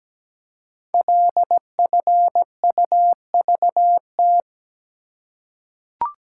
Morse code, heard as a keyed tone, about 17 wpm, 700 Hz. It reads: LFUVT
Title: Code Groups - 5 Letters - 17wpm